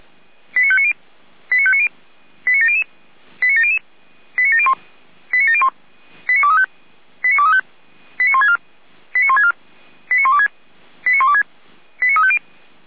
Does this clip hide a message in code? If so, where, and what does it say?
Yes: five-tone selective call (ZVEI) from 0.6 to 12.4 s, heard as quick runs of five tones, tones of about 70 ms. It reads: 89590 89590 89700 89700 89701 89701 89235 89235 89164 89164 89126 89126 89350